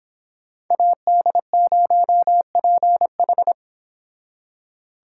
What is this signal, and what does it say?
Morse code, heard as a keyed tone, about 26 wpm, 700 Hz. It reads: AD0PH